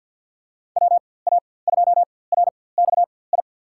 Morse code, heard as a keyed tone, about 49 wpm, spaced as for 23 wpm, 700 Hz. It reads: WA2RXI
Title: USA Call Signs - 6 Characters - 4x Character Spacing - 50wpm